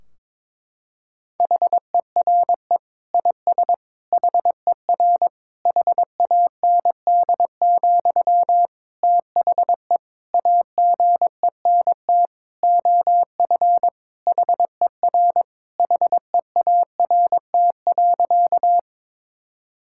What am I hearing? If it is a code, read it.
Morse code, heard as a keyed tone, about 22 wpm, 700 Hz. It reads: HERE IS HER HAND, THE AGENT OF HER HEART.
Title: Sentences from Top 1600 Words 22wpm